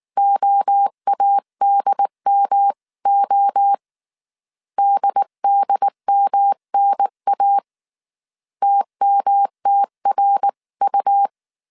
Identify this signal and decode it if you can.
Morse code, heard as a keyed tone, about 19 wpm, 800 Hz. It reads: OADMO DDMNA TMTRU